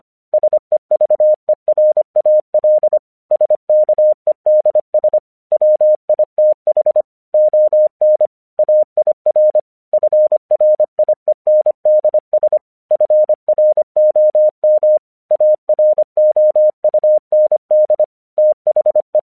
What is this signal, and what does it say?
Morse code, heard as a keyed tone, about 24 wpm, 610 Hz. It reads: SEVERAL SKEDS WITH ON AIR FRIENDS FROM AROUND THE
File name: 1 English 25wpm / AQC13